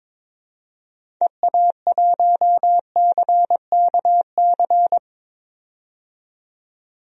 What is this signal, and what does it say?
Morse code, heard as a keyed tone, about 22 wpm, 700 Hz. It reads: EA1CKC
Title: DXCC-100 Call Signs - 6 Characters - 22wpm